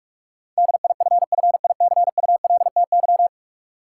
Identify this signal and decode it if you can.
Morse code, heard as a keyed tone, about 46 wpm, 700 Hz. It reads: DIFFICULTY
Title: Sentences from Top 1500 Words - New Word Drill - 45wpm